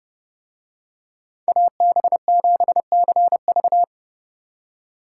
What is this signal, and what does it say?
Morse code, heard as a keyed tone, about 30 wpm, 700 Hz. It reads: AB7CV